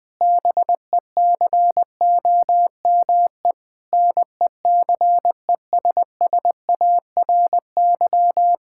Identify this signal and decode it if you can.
Morse code, heard as a keyed tone, about 20 wpm, 700 Hz. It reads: BECOME NECESSARY
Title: Sentences from Top 1000 Words - Review Part 5 - 20wpm